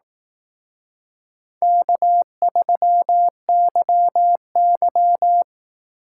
Morse code, heard as a keyed tone, about 18 wpm, 700 Hz. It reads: K3YY
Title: Speed Racing - USA Call Signs 15wpm